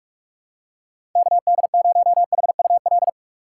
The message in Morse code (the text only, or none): KD0HUL